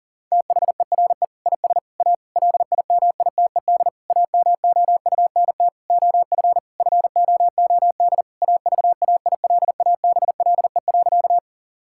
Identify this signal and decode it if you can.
Morse code, heard as a keyed tone, about 40 wpm, 700 Hz. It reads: THERE IS A LIMITED AMOUNT OF FOOD AVAILABLE.
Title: Sentences from Top 1000 Words - Review Part 6 - 40wpm